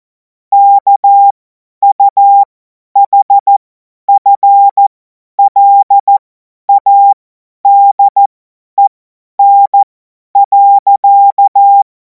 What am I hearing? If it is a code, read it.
Morse code, heard as a keyed tone, about 14 wpm, spaced as for 10 wpm, 800 Hz. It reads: KUHFLADEN.
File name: Morsekurs / Witze / Witz 30. Juni 2026 50 F70